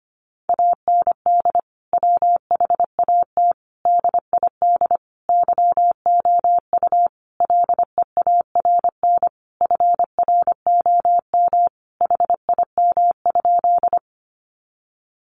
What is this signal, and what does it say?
Morse code, heard as a keyed tone, about 25 wpm, 700 Hz. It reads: AND WHAT DID YOU LEARN FROM HIM?
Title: Sentences from Top 300 Words 25wpm